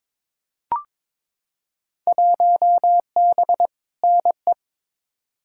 Morse code, heard as a keyed tone, about 22 wpm, 700 Hz. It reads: 1B NE
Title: Field Day Exchanges Encore - 22wpm